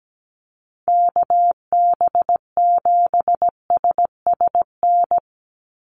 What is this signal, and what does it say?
Morse code, heard as a keyed tone, about 17 wpm, 700 Hz. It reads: KB7SSN
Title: USA Call Signs 17wpm